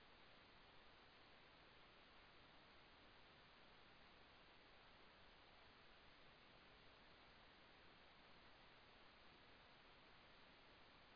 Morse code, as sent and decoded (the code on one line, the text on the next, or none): none